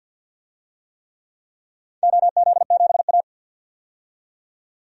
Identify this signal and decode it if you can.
Morse code, heard as a keyed tone, about 47 wpm, 700 Hz. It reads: OZ7A
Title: DXCC-100 Call Signs - 4 Characters - 50wpm